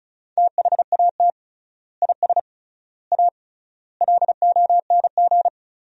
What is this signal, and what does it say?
Morse code, heard as a keyed tone, about 36 wpm, 700 Hz. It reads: THAT IS A LONG
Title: 3x Word-Spacing - Sentences from Top 100 Words - 35wpm